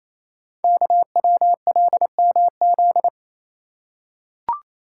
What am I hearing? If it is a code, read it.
Morse code, heard as a keyed tone, about 28 wpm, 700 Hz. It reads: KWLMZ
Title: Code Groups - 5 Letter:Numbers - 28wpm